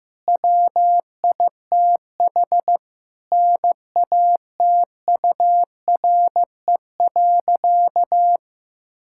Morse code, heard as a keyed tone, about 15 wpm, 700 Hz. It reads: WITH NATURE.
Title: Sentences from Top 1000 Words - Review Part 6 - 15wpm